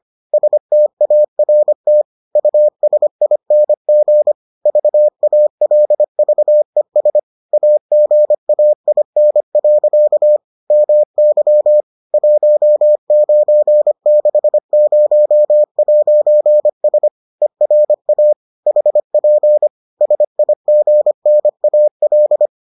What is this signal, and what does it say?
Morse code, heard as a keyed tone, about 25 wpm, 600 Hz. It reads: START USING VALVES AGAIN. MY 1960'S ERA HP SIGNAL